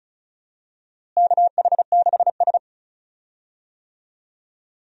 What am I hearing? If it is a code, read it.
Morse code, heard as a keyed tone, about 36 wpm, 700 Hz. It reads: KH6S
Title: DXCC-100 Call Signs - 4 Characters - Repeated 3x - 35wpm